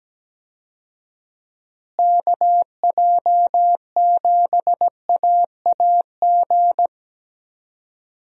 Morse code, heard as a keyed tone, about 17 wpm, 700 Hz. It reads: KJ7AAG
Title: USA Call Signs - 6 Characters - Repeated 2x - 17wpm